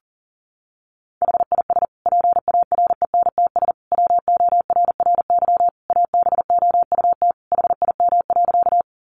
Morse code, heard as a keyed tone, about 37 wpm, 700 Hz. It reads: HIS PARENTS WORRY ABOUT HIM.
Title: Easy Phrases 40wpm